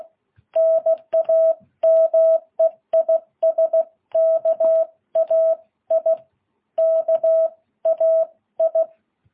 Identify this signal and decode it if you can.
Morse code, heard as a keyed tone, about 16 wpm, 660 Hz. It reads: NAME IS KAI KAI
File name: jh8cbh